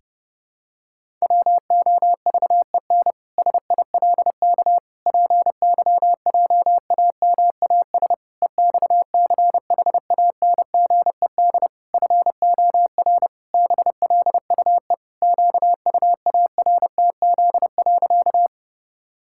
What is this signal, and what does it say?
Morse code, heard as a keyed tone, about 30 wpm, 700 Hz. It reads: WOVEN SILK PYJAMAS EXCHANGED FOR BLUE QUARTZ.
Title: Pangram Challenge v2 - 30wpm